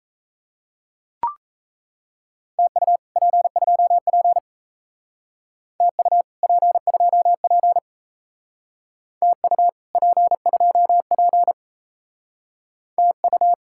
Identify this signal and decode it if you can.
Morse code, heard as a keyed tone, about 36 wpm, 700 Hz. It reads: TU P2P TU P2P TU P2P TU